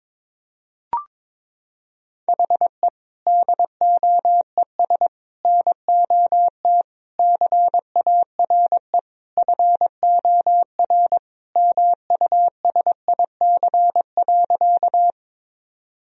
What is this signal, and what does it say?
Morse code, heard as a keyed tone, about 22 wpm, 700 Hz. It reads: HE DOES NOT CARE FOR MUSIC.
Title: Sentences from Top 1000 Words - Review Part 1 - 22wpm